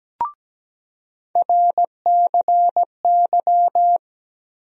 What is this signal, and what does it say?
Morse code, heard as a keyed tone, about 17 wpm, 700 Hz. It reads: RCY
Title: USA Call Sign Postfixes - 3 Letters 17wpm